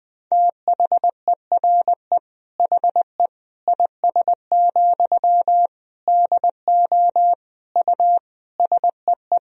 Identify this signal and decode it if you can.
Morse code, heard as a keyed tone, about 21 wpm, 700 Hz. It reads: THERE HE IS, DO U SEE
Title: Sentences from Top 100 Words and Abbreviations 20wpm